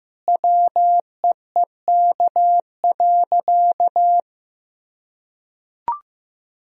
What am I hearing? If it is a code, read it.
Morse code, heard as a keyed tone, about 15 wpm, 700 Hz. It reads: WEEK.
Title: Sentences from Top 1000 Words - Review Part 2 - 15wpm